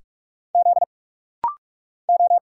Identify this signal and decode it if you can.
Morse code, heard as a keyed tone, about 46 wpm, 690 Hz. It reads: Z O
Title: Single Letters - Rapid-Fire 45wpm